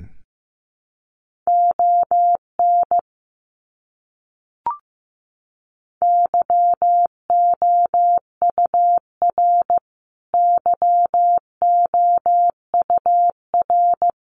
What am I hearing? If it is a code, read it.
Morse code, heard as a keyed tone, about 15 wpm, 700 Hz. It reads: ON YOUR YOUR
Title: Top 200 Words Repeated 15wpm